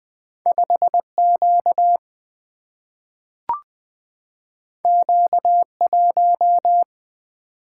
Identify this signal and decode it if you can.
Morse code, heard as a keyed tone, about 20 wpm, 700 Hz. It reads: 5Q Q1
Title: Single-Letter Single-Number 20wpm